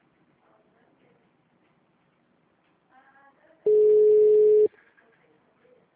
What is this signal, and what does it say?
Ringback tone (Europe, 425 Hz).
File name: out- -2020- - - .9764